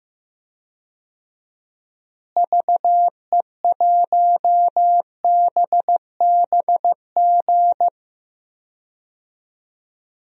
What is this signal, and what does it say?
Morse code, heard as a keyed tone, about 16 wpm, 700 Hz. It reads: VE1BBG
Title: DXCC-100 Call Signs - 6 Characters - Repeated 3x - 15wpm